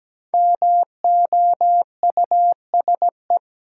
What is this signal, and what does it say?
Morse code, heard as a keyed tone, about 17 wpm, 700 Hz. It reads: MOUSE
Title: Sentences from Top 700 Words - New Word Drill - 17wpm